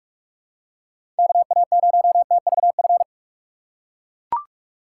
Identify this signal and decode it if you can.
Morse code, heard as a keyed tone, about 44 wpm, 700 Hz. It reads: KA0TVF
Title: USA Call Signs - 6 Characters - 2x Character Spacing - 45wpm